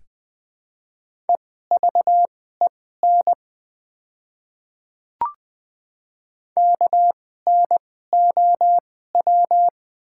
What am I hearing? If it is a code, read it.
Morse code, heard as a keyed tone, about 20 wpm, spaced as for 14 wpm, 690 Hz. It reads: EVEN KNOW
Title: 2x Character Spacing - Top 300 Words - 20wpm